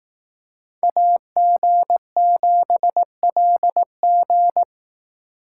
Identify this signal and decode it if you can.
Morse code, heard as a keyed tone, about 18 wpm, 700 Hz. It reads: AG7LG
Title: Speed Racing - USA 2x2 Call Signs 15wpm